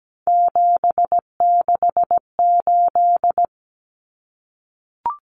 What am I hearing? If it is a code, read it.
Morse code, heard as a keyed tone, about 17 wpm, 700 Hz. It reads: 768
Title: Common Cut Numbers - 3-Digit 17wpm